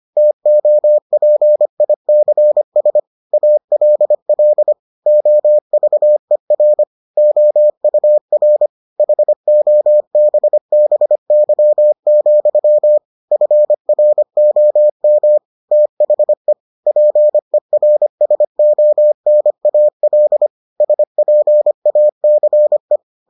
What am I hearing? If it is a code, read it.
Morse code, heard as a keyed tone, about 25 wpm, 600 Hz. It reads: TOPICS ALL OVER OUR HOBBY, FROM THE PERSONAL SPACE